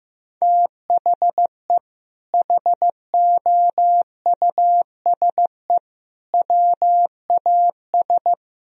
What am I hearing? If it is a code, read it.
Morse code, heard as a keyed tone, about 15 wpm, 700 Hz. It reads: THE HOUSE WAS
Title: Sentences from Top 1000 Words - Review Part 3 - 15wpm